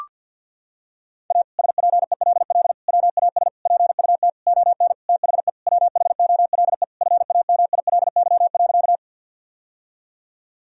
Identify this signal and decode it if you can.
Morse code, heard as a keyed tone, about 52 wpm, 700 Hz. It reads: A SPELL WAS PUT ON THE WHOLE FAMILY.
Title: Sentences from Top 1000 Words - Review Part 1 - 50wpm